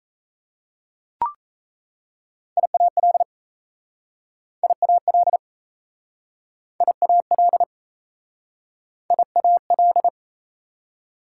Morse code, heard as a keyed tone, about 38 wpm, 700 Hz. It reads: IAL IAL IAL IAL